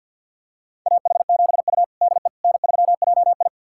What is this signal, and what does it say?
Morse code, heard as a keyed tone, about 51 wpm, 700 Hz. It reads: AH7U DE N3JI
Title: USA Callsign DE Callsign - 4 Characters - 50wpm